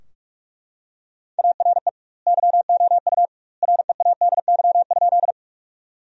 Morse code, heard as a keyed tone, about 43 wpm, 700 Hz. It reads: ARE YOU READY?
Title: 2x Word-Spacing - Sentences from Top 400 Words - 45wpm